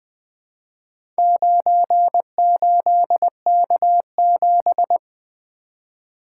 Morse code, heard as a keyed tone, about 20 wpm, 700 Hz. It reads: 98K7
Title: Four Letter-Number Combinations - 20wpm